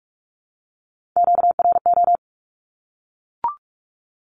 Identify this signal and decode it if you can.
Morse code, heard as a keyed tone, about 44 wpm, 700 Hz. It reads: QRO
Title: Q-Codes 45wpm